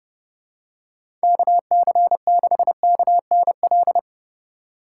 Morse code, heard as a keyed tone, about 31 wpm, 700 Hz. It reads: KC6KNL
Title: USA Call Signs - 6 Characters - 3x Character Spacing - 30wpm